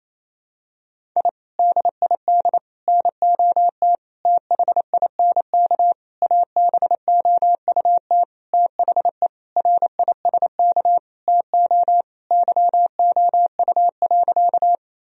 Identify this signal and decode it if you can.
Morse code, heard as a keyed tone, about 28 wpm, 700 Hz. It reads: I DID NOT THINK ABOUT THE RISK TO YOU.
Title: Sentences from Top 1000 Words - Review Part 4 - 28wpm